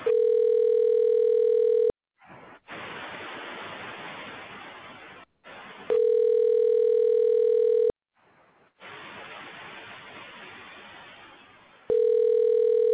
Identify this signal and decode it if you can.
Ringback tone (North America), 440 + 480 Hz.